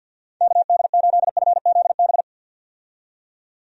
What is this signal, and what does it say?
Morse code, heard as a keyed tone, about 51 wpm, 700 Hz. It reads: KD8FZB
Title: USA Call Signs - 6 Characters - 50wpm